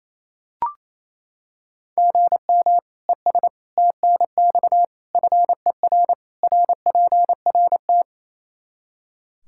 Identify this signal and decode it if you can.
Morse code, heard as a keyed tone, about 27 wpm, 700 Hz. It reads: GM ES TNX FER RPRT
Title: K1USN SST QSO Practice 28wpm